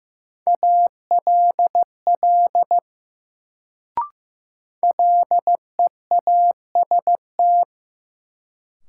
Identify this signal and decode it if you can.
Morse code, heard as a keyed tone, about 15 wpm, 700 Hz. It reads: ALL LEAST
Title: Sets of 2 Words - New Word Drill - 15wpm